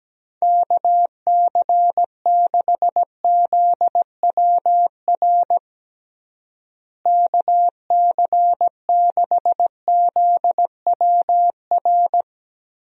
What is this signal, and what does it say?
Morse code, heard as a keyed tone, about 17 wpm, 700 Hz. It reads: KC6ZWR KC6ZWR
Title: USA Call Signs - 6 Characters - Repeated 3x - 17wpm